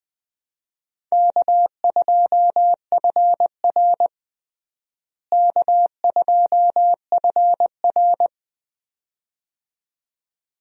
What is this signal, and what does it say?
Morse code, heard as a keyed tone, about 21 wpm, 700 Hz. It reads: K2FR K2FR
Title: USA Call Signs - 4 Characters - Repeated 2x - 20wpm